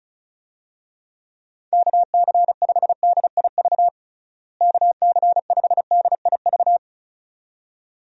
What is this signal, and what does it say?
Morse code, heard as a keyed tone, about 34 wpm, 700 Hz. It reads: KC5DIV KC5DIV
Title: USA Call Signs - 6 Characters - Repeated 2x - 35wpm